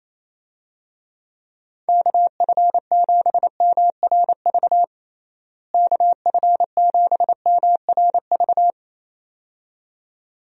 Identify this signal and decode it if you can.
Morse code, heard as a keyed tone, about 28 wpm, 700 Hz. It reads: KF7MRV KF7MRV
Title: USA Call Signs - 6 Characters - Repeated 2x - 28wpm